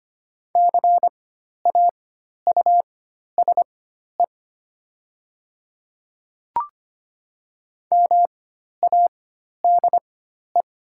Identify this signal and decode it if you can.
Morse code, heard as a keyed tone, about 25 wpm, spaced as for 12 wpm, 700 Hz. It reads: CAUSE MADE
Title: 4x Character Spacing - Top 200 Words - 25wpm